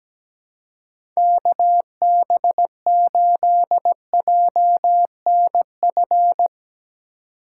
Morse code, heard as a keyed tone, about 17 wpm, 700 Hz. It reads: KB8JNF
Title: USA Call Signs - 6 Characters - 4x Character Spacing - 17wpm